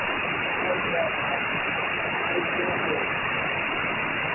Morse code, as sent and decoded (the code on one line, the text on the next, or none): none